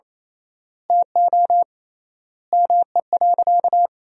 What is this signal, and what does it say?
Morse code, heard as a keyed tone, about 28 wpm, 700 Hz. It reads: TO ME.
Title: 3x Word-Spacing - Sentences from Top 500 Words - 28wpm